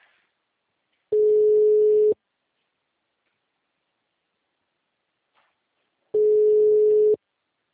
Ringback tone (Europe, 425 Hz).
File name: out- -2007- - - .12207